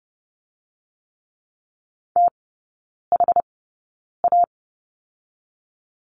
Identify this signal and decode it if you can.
Morse code, heard as a keyed tone, about 30 wpm, 700 Hz. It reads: T H A
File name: Lesson 19e - Letter w - Three Characters 30wpm